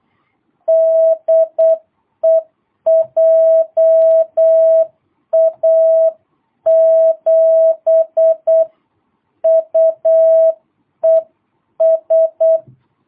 Morse code, heard as a keyed tone, about 8 wpm, 650 Hz. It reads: DE JA7UES
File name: jh8cbh / 2022cw